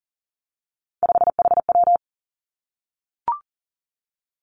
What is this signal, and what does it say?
Morse code, heard as a keyed tone, about 40 wpm, 700 Hz. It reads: 5HW
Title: Three Challenging Letter-Number Combinations 40wpm